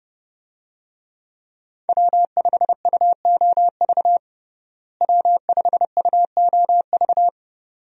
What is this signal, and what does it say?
Morse code, heard as a keyed tone, about 30 wpm, 700 Hz. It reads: W5UOV W5UOV
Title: USA Call Signs - 5 Characters - Repeated 2x - 30wpm